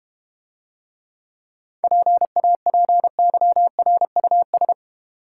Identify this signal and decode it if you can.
Morse code, heard as a keyed tone, about 32 wpm, 700 Hz. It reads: PAPYRUS